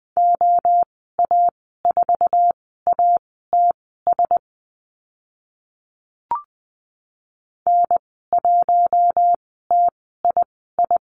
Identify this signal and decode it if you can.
Morse code, heard as a keyed tone, about 20 wpm, spaced as for 14 wpm, 700 Hz. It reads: OA4ATS N1TII
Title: 062 Lesson 11g - Letter h - Call Signs 20wpm